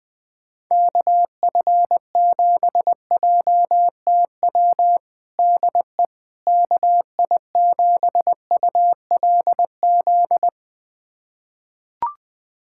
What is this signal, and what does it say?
Morse code, heard as a keyed tone, about 21 wpm, 700 Hz. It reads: KF7JTW DE KI7ULZ
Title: USA Callsign DE Callsign 20wpm